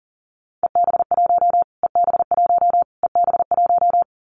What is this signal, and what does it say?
Morse code, heard as a keyed tone, about 40 wpm, 700 Hz. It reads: EB1 EB1 EB1
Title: Callsign Prefixes for Top 25 DXCC - Repeated 3x - 40wpm